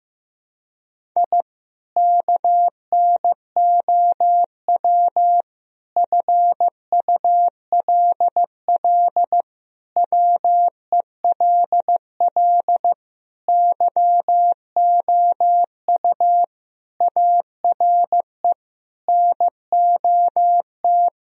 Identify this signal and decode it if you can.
Morse code, heard as a keyed tone, about 15 wpm, 700 Hz. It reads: I KNOW FULL WELL YOU ARE NOT
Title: Sentences from Top 1000 Words - Review Part 3 - 15wpm